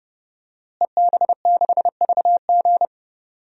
Morse code, heard as a keyed tone, about 30 wpm, 700 Hz. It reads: EB6VG
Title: DXCC-100 Call Signs - 5 Characters - Repeated 2x - 30wpm